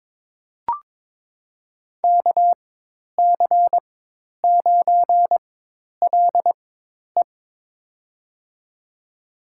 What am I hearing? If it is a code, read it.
Morse code, heard as a keyed tone, about 22 wpm, spaced as for 10 wpm, 700 Hz. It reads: KC9LE
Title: USA Call Signs - 5 Characters - 4x Character Spacing - 22wpm